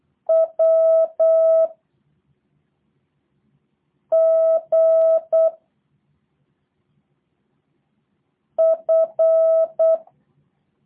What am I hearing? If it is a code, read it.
Morse code, heard as a keyed tone, about 8 wpm, 650 Hz. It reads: W G F
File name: jh8cbh / 2023cw / review1-4